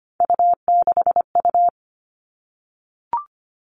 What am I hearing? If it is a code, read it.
Morse code, heard as a keyed tone, about 25 wpm, 700 Hz. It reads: U6U
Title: All Cut Numbers - 3-Digit 25wpm